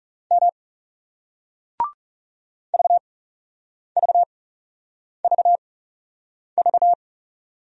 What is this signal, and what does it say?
Morse code, heard as a keyed tone, about 39 wpm, 700 Hz. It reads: M V V V V